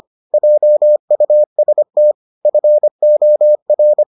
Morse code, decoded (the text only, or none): JUST FOR